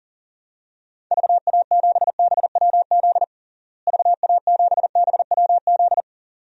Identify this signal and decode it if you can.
Morse code, heard as a keyed tone, about 40 wpm, 700 Hz. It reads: VA7BWZ VA7BWZ
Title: DXCC-100 Call Signs - 6 Characters - Repeated 2x - 40wpm